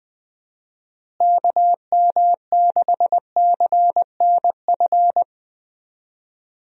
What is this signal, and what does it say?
Morse code, heard as a keyed tone, about 21 wpm, 700 Hz. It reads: KM6CNF
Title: USA Call Signs - 6 Characters - Repeated 2x - 20wpm